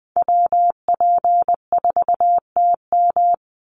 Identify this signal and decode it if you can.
Morse code, heard as a keyed tone, about 20 wpm, 700 Hz. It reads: WP4TM
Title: USA Call Signs 20wpm